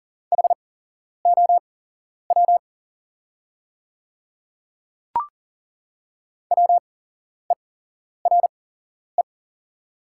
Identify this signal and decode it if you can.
Morse code, heard as a keyed tone, about 39 wpm, spaced as for 11 wpm, 700 Hz. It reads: HOW WERE
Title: 8x Character Spacing - Top 100 Words - 40wpm